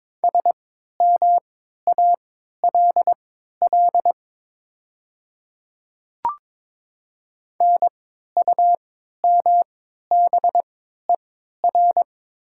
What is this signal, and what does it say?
Morse code, heard as a keyed tone, about 22 wpm, spaced as for 12 wpm, 690 Hz. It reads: SMALL NUMBER
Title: 3x Character Spacing - Top 200 Words - 22wpm